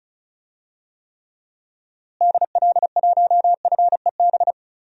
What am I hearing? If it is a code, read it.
Morse code, heard as a keyed tone, about 36 wpm, 700 Hz. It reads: DL1FEB
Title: DXCC-100 Call Signs - 6 Characters - Repeated 3x - 35wpm